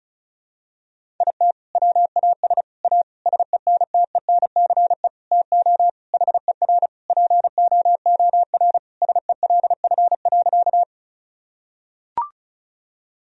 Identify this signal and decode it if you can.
Morse code, heard as a keyed tone, about 36 wpm, 700 Hz. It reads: IT WAS A SENTENCE TO HER POOR SELF.
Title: Sentences from Top 1000 Words - Review Part 1 - 35wpm